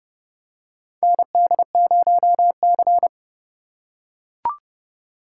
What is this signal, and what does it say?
Morse code, heard as a keyed tone, about 30 wpm, 700 Hz. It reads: ND0C
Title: USA Call Signs - 4 Characters - 2x Character Spacing - 30wpm